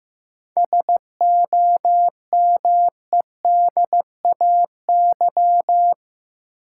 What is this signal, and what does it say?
Morse code, heard as a keyed tone, about 15 wpm, 700 Hz. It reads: SOMEDAY
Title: Sentences from Top 1600 Words - New Word Drill - 15wpm